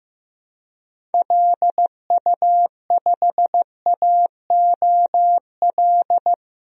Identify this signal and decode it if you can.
Morse code, heard as a keyed tone, about 15 wpm, 700 Hz. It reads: LU5AOL
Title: DXCC-100 Call Signs - 6 Characters - 15wpm